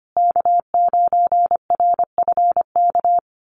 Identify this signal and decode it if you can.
Morse code, heard as a keyed tone, about 25 wpm, 700 Hz. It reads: K9RFK
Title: USA Call Signs Repeated 3x 25wpm